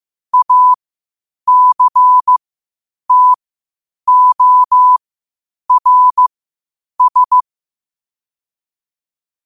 Morse code, heard as a keyed tone, about 15 wpm, spaced as for 8 wpm, 1000 Hz. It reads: ACTORS
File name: qotdfiles / 8wpm